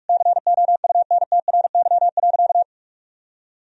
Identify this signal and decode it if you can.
Morse code, heard as a keyed tone, about 46 wpm, 700 Hz. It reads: COUNTRY.
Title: Sentences from Top 300 Words and Abbreviations 45wpm